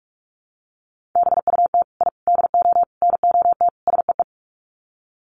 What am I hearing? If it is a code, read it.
Morse code, heard as a keyed tone, about 44 wpm, 700 Hz. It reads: BUT I DO NOT SEE
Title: Sets of 5 Words Encore 45wpm